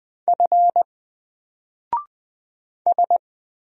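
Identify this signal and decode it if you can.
Morse code, heard as a keyed tone, about 20 wpm, 700 Hz. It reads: F S
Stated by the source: Single letter and number Morse drill at 20 wpm, from N4LCD.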